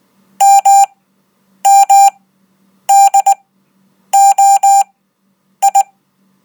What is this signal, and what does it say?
Morse code, heard as a keyed tone, about 19 wpm, spaced as for 8 wpm, 790 Hz. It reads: MMDOI
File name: Code